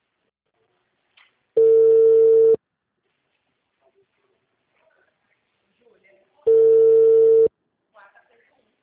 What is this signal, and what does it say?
Ringback tone (China, 450 Hz), with speech.
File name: out- -2002- - - .20